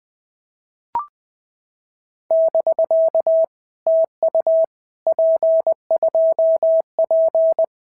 Morse code, heard as a keyed tone, about 20 wpm, 650 Hz. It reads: <BK> TU P2P
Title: POTA QSO Practice - 20wpm